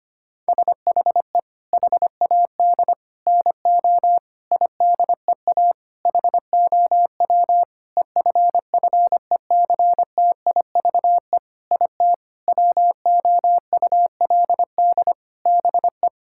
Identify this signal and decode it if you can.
Morse code, heard as a keyed tone, about 25 wpm, 700 Hz. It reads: SHE HAD NO IDEA HOW EFFECTIVE IT WOULD BE
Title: Sentences from Top 1000 Words - Review Part 6 - 25wpm